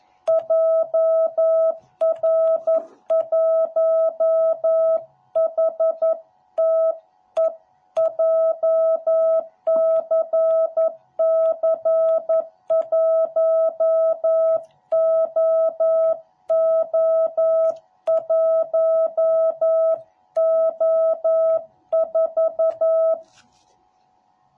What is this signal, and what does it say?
Morse code, heard as a keyed tone, about 11 wpm, 660 Hz. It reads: JR1HTE JCC1OO1O4